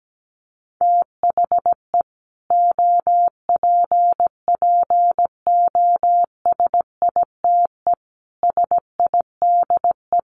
Morse code, heard as a keyed tone, about 17 wpm, 700 Hz. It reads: THE OPPOSITE SIDE
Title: Sets of 5 Words Encore 17wpm